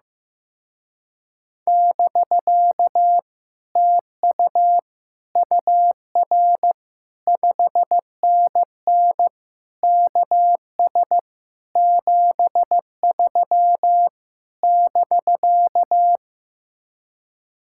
Morse code, heard as a keyed tone, about 15 wpm, 700 Hz. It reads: <BK> TU UR 5NN KS 73 <BK>
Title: Speed-Racing Common QSO Phrases Encore - 15wpm